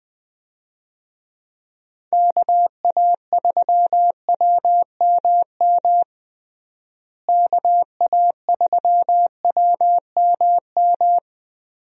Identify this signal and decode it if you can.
Morse code, heard as a keyed tone, about 21 wpm, 700 Hz. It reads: KA3WMM KA3WMM
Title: USA Call Signs - 6 Characters - Repeated 2x - 20wpm